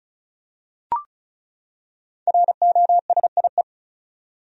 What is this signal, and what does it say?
Morse code, heard as a keyed tone, about 36 wpm, 700 Hz. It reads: ROSIE
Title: USA Common Names - 35wpm